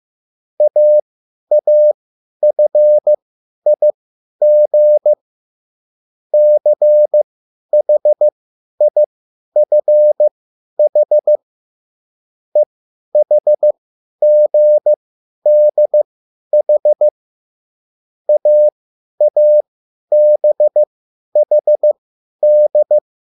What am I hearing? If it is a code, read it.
Morse code, heard as a keyed tone, about 15 wpm, spaced as for 10 wpm, 600 Hz. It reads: AAFIG CHIFH EHGDH AABHD
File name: QSL / CW lesson